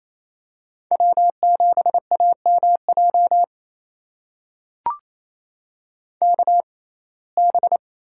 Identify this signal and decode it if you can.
Morse code, heard as a keyed tone, about 28 wpm, spaced as for 20 wpm, 700 Hz. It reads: W7AMJ KB
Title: USA Call Signs - 5 Characters - 6x Character Spacing - 28wpm